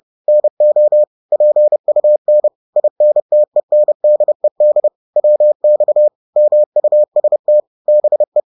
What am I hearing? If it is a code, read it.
Morse code, heard as a keyed tone, about 30 wpm, 600 Hz. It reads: NO PUN INTENDED WX MUST BE